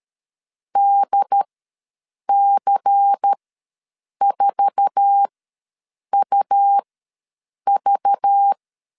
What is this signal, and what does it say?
Morse code, heard as a keyed tone, about 13 wpm, spaced as for 7 wpm, 800 Hz. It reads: DC4UV
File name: LEZIONI / 8 YJ45 / 8 YJ4 ws 12wpm 3249